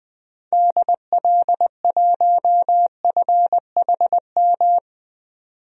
Morse code, heard as a keyed tone, about 21 wpm, 700 Hz. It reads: DL1FHM
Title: DXCC-100 Call Signs - 6 Characters - 20wpm